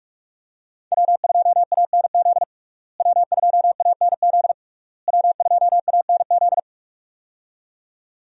Morse code, heard as a keyed tone, about 46 wpm, 700 Hz. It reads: W2ANZ W2ANZ W2ANZ
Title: USA Call Signs - 5 Characters - Repeated 3x - 45wpm